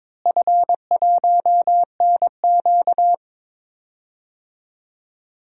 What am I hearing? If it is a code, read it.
Morse code, heard as a keyed tone, about 22 wpm, 700 Hz. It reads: F1NQ
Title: DXCC-100 Call Signs - 4 Characters - Repeated 2x - 22wpm